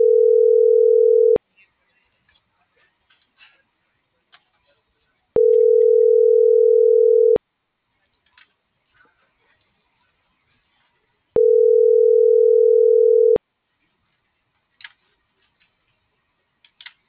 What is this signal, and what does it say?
Ringback tone (North America), 440 + 480 Hz.